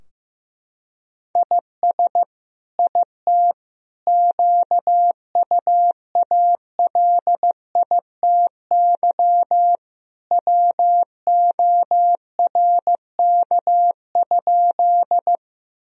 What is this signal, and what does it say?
Morse code, heard as a keyed tone, about 15 wpm, 700 Hz. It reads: IS IT QUALITY WORK?
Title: Sentences from Top 1000 Words - Review Part 4 - 15wpm